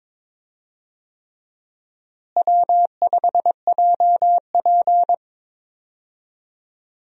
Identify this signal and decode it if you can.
Morse code, heard as a keyed tone, about 21 wpm, 700 Hz. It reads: W5JP
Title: USA Call Signs - 4 Characters - Repeated 3x - 22wpm